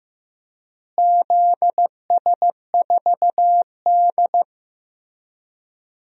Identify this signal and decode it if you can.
Morse code, heard as a keyed tone, about 15 wpm, 700 Hz. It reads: ZS4D
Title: DXCC-100 Call Signs - 4 Characters - Repeated 3x - 15wpm